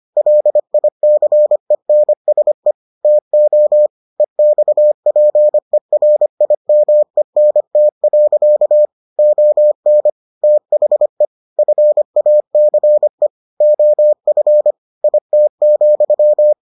Morse code, heard as a keyed tone, about 25 wpm, 600 Hz. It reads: LICENSE TO EXPERIMENT. ON THE FACE OF IT,